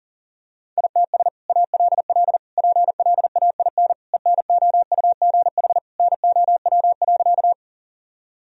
Morse code, heard as a keyed tone, about 37 wpm, 700 Hz. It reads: ITS ALL PLAIN ENOUGH NOW.
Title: Sentences from Top 1000 Words - Review Part 2 - 40wpm